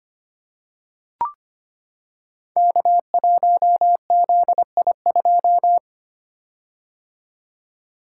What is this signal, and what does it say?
Morse code, heard as a keyed tone, about 25 wpm, 700 Hz. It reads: K1ZI2
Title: Code Groups - 5 Letter:Numbers - 25wpm